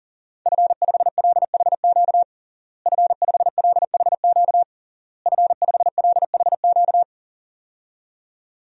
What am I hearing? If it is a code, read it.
Morse code, heard as a keyed tone, about 40 wpm, 700 Hz. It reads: F5LHQ F5LHQ F5LHQ
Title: DXCC-100 Call Signs - 5 Characters - Repeated 3x - 40wpm